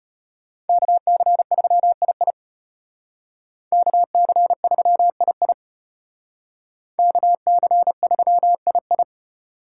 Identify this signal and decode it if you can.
Morse code, heard as a keyed tone, about 33 wpm, 700 Hz. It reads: KC3II KC3II KC3II